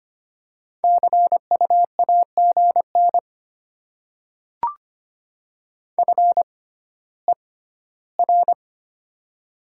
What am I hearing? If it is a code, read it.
Morse code, heard as a keyed tone, about 25 wpm, spaced as for 14 wpm, 700 Hz. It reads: CU AGN FER
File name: Top 20 QSO Elements - 6x Character Spacing - 25wpm